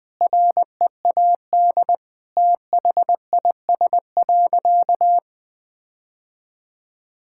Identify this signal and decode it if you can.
Morse code, heard as a keyed tone, about 20 wpm, 700 Hz. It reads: READ THIS.
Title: Sentences with 8-Word Phrases - 20wpm